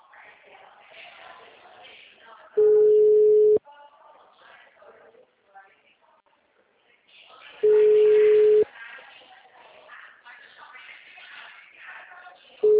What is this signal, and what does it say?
Ringback tone (Europe, 425 Hz), with speech.
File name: out- -2021- - - .21935